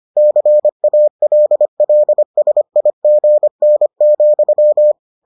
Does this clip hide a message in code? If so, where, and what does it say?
Yes: Morse code from 0.2 to 4.9 s, heard as a keyed tone, about 25 wpm, 600 Hz. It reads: CALLSIGN,